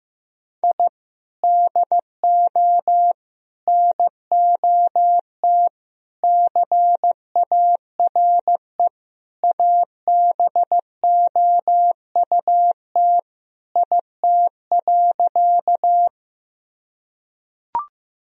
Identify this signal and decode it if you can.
Morse code, heard as a keyed tone, about 16 wpm, 700 Hz. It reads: I DO NOT CARE ABOUT IT.
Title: Sentences from Top 1000 Words - Review Part 1 - 15wpm